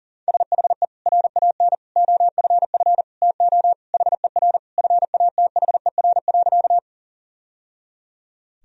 Morse code, heard as a keyed tone, about 40 wpm, 700 Hz. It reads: SHE RAN OFF TO HER FATHER.
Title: Sentences from Top 1000 Words - Review Part 3 - 40wpm